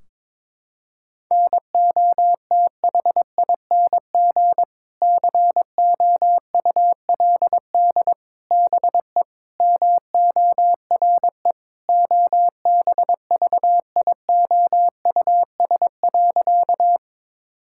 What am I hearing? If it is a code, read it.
Morse code, heard as a keyed tone, about 22 wpm, 700 Hz. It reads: NOTHING COULD BE MORE OBVIOUS.
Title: Sentences from Top 1600 Words 22wpm